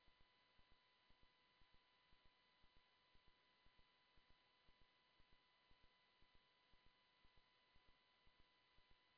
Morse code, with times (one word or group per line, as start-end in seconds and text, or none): none